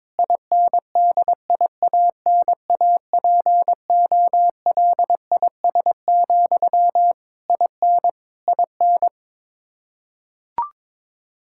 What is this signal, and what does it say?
Morse code, heard as a keyed tone, about 22 wpm, 700 Hz. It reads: INDIANAPOLIS, IN IN
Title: Practice QSO - Exchange 1 - Beginning - 22wpm